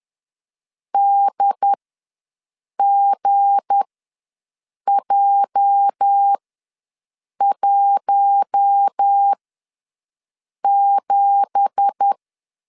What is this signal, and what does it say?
Morse code, heard as a keyed tone, about 10 wpm, spaced as for 6 wpm, 800 Hz. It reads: DGJ17
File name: LEZIONI / 9 2867 ws 10wpm 4215